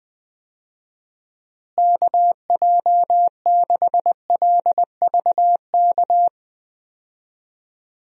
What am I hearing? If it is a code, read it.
Morse code, heard as a keyed tone, about 20 wpm, 700 Hz. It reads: KJ6LVK
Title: USA Call Signs - 6 Characters - Repeated 3x - 20wpm